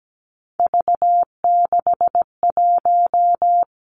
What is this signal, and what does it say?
Morse code, heard as a keyed tone, about 17 wpm, 700 Hz. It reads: V61
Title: Three Challenging Letter-Number Combinations 17wpm